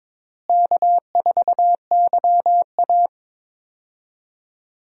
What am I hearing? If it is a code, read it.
Morse code, heard as a keyed tone, about 22 wpm, 700 Hz. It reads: K4YA
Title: USA Call Signs - 4 Characters - Repeated 2x - 22wpm